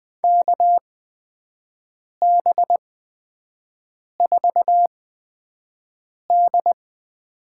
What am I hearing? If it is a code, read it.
Morse code, heard as a keyed tone, about 20 wpm, spaced as for 5 wpm, 700 Hz. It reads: KB4D
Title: USA Call Signs - 6 Characters - 8x Character Spacing - 20wpm